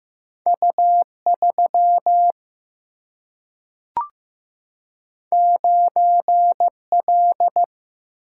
Morse code, heard as a keyed tone, about 15 wpm, 700 Hz. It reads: U3 9L
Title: Single-Letter Single-Number 15wpm